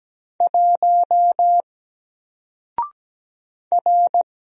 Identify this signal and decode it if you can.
Morse code, heard as a keyed tone, about 17 wpm, 700 Hz. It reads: 1 R
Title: Single Letter-Number Alt 17wpm